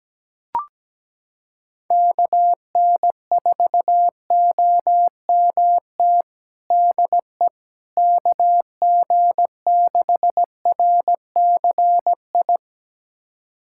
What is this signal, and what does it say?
Morse code, heard as a keyed tone, about 17 wpm, 700 Hz. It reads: KN4OMT DE KG6RCI
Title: USA Callsign DE Callsign - 6 Characters - 17wpm